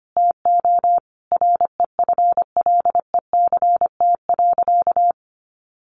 Morse code, heard as a keyed tone, about 25 wpm, 700 Hz. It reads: TO REFLECT.
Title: Sentences from Top 1500 Words 25wpm